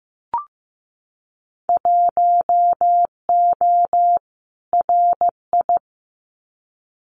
Morse code, heard as a keyed tone, about 15 wpm, 700 Hz. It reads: 1O RI
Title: Winter Field Day Exchanges 15wpm